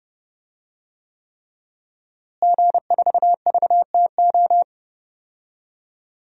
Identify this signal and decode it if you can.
Morse code, heard as a keyed tone, about 30 wpm, 700 Hz. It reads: G4VTO
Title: DXCC-100 Call Signs - 5 Characters - Repeated 3x - 30wpm